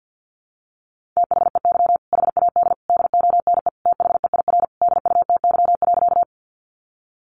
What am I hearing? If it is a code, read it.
Morse code, heard as a keyed tone, about 48 wpm, 700 Hz. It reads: THEY HAD DONE THEIR DUTY.
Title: Sentences from Top 1500 Words 50wpm